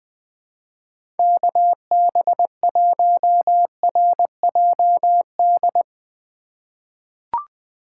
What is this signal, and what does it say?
Morse code, heard as a keyed tone, about 20 wpm, 700 Hz. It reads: KB1RJD
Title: 2023 SOTA Chaser Call Signs - 20wpm